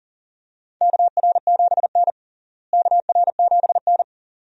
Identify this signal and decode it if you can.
Morse code, heard as a keyed tone, about 37 wpm, 700 Hz. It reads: KR7N KR7N
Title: USA Call Signs - 4 Characters - Repeated 2x - 40wpm